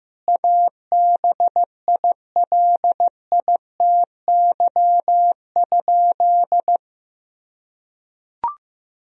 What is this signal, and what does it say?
Morse code, heard as a keyed tone, about 15 wpm, 700 Hz. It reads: ABILITY?
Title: Sentences from Top 1000 Words - Review Part 4 - 15wpm